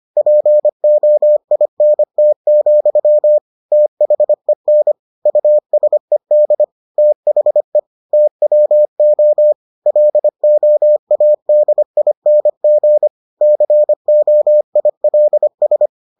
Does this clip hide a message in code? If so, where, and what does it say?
Yes: Morse code from 0.2 to 15.9 s, heard as a keyed tone, about 25 wpm, 600 Hz. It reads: POINT, THEN USED THE TWO LOADING COILS